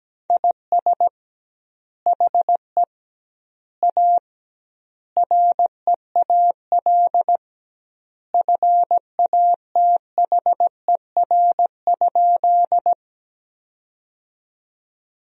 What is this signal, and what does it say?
Morse code, heard as a keyed tone, about 17 wpm, 700 Hz. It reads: IS HE A REAL FATHER?
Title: 2x Word-Spacing - Sentences from Top 300 Words - 17wpm